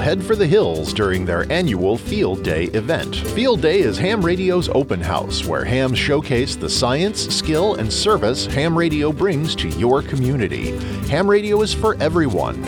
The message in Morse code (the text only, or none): none